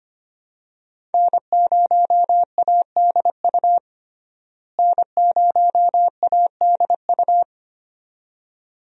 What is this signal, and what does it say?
Morse code, heard as a keyed tone, about 25 wpm, 700 Hz. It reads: N0ADU N0ADU
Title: USA Call Signs - 5 Characters - Repeated 2x - 25wpm